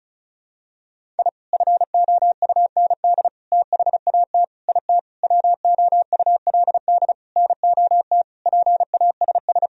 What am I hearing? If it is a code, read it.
Morse code, heard as a keyed tone, about 34 wpm, 700 Hz. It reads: I FOUND THAT IT WOULD NOT PASS
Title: Sentences from Top 1000 Words - Review Part 1 - 35wpm